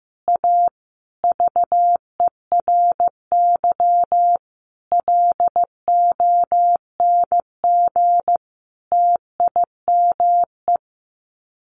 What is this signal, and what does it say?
Morse code, heard as a keyed tone, about 15 wpm, 700 Hz. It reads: A VERY LONG TIME
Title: Sets of 5 Words 15wpm